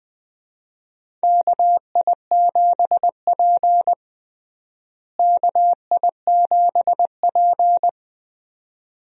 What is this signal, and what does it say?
Morse code, heard as a keyed tone, about 20 wpm, 700 Hz. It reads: KI7P KI7P
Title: USA Call Signs - 4 Characters - Repeated 2x - 20wpm